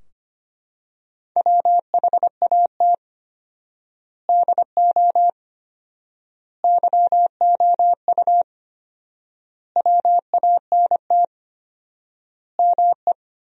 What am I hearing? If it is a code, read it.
Morse code, heard as a keyed tone, about 25 wpm, 700 Hz. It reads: WHAT DO YOU WANT ME
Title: 4x Word-Spacing - Sentences from Top 200 Words - 25wpm